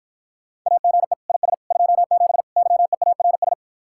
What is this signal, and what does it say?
Morse code, heard as a keyed tone, about 52 wpm, 700 Hz. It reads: AGE IS 27 YEARS